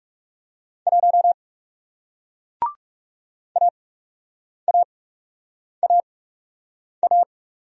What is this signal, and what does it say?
Morse code, heard as a keyed tone, about 42 wpm, spaced as for 8 wpm, 700 Hz. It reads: 1 AAAA